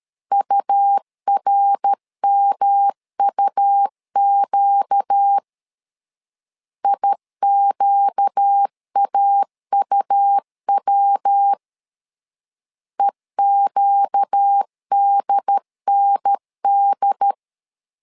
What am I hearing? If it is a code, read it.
Morse code, heard as a keyed tone, about 13 wpm, 800 Hz. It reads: URMUQ IQAUW EQDND
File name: LEZIONI / 4 Q 12wpm 1833